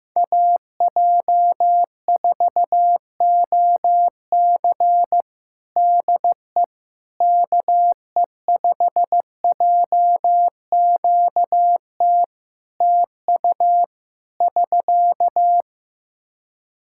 Morse code, heard as a keyed tone, about 15 wpm, 700 Hz. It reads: AJ4OC DE KE5JQT TU <SK>
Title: K1USN SST QSO Practice 15wpm